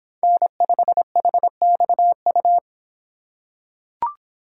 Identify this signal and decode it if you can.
Morse code, heard as a keyed tone, about 26 wpm, 700 Hz. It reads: N5HXU